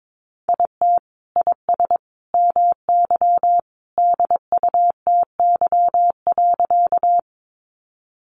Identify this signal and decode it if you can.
Morse code, heard as a keyed tone, about 22 wpm, 700 Hz. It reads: IT IS MY DUTY.
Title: Sentences from Top 1500 Words 22wpm